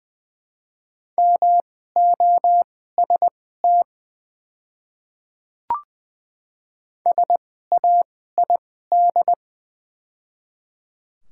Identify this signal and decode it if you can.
Morse code, heard as a keyed tone, about 20 wpm, spaced as for 15 wpm, 700 Hz. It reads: MOST SAID
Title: 2x Character Spacing - Top 100 Words - 20wpm